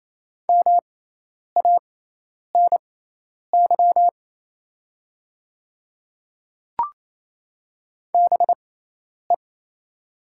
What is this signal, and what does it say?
Morse code, heard as a keyed tone, about 28 wpm, spaced as for 10 wpm, 700 Hz. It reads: MANY BE
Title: 6x Character Spacing - Top 100 Words - 28wpm